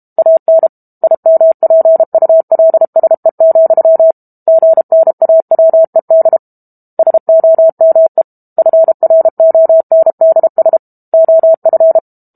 Morse code, heard as a keyed tone, about 32 wpm, 650 Hz. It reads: AN IMPULSE, GNAWED SOME FRONDS OF
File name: Book Two / 06-Chapter 6 - 32 WPM